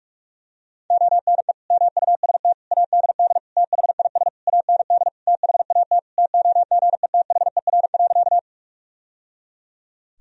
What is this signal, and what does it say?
Morse code, heard as a keyed tone, about 46 wpm, 700 Hz. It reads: ONE MUST ADD THIS AND THAT TOGETHER.
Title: Sentences from Top 1000 Words - Review Part 1 - 45wpm